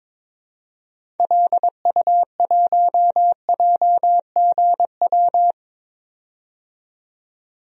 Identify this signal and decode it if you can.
Morse code, heard as a keyed tone, about 22 wpm, 700 Hz. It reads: LU1JGW
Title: DXCC-100 Call Signs - 6 Characters - 22wpm